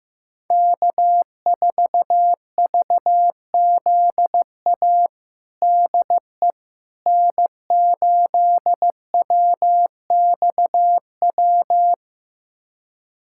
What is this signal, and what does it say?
Morse code, heard as a keyed tone, about 15 wpm, 700 Hz. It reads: K4VZA DE N8WXW
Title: USA Callsign DE Callsign - 5 Characters - 15wpm